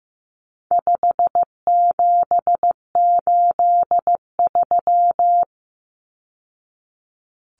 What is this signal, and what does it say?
Morse code, heard as a keyed tone, about 15 wpm, 700 Hz. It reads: 5783
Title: Numbers - 4-Digit 15wpm